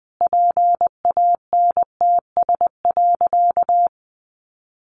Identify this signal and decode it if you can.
Morse code, heard as a keyed tone, about 20 wpm, 700 Hz. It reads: PANTS.
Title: Additional Easy Phrases 20wpm